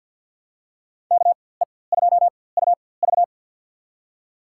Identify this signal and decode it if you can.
Morse code, heard as a keyed tone, about 46 wpm, spaced as for 23 wpm, 700 Hz. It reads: KE2UV
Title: USA Call Signs - 5 Characters - 4x Character Spacing - 50wpm